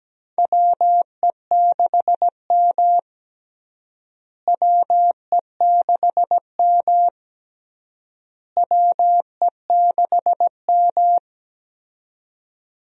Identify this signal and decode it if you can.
Morse code, heard as a keyed tone, about 17 wpm, 700 Hz. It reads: WE6M WE6M WE6M
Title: USA Call Signs - 4 Characters - Repeated 3x - 17wpm